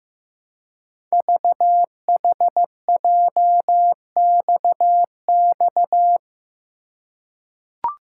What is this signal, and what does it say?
Morse code, heard as a keyed tone, about 15 wpm, 700 Hz. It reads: VHJXX
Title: Code Groups - 5 Letters - 15wpm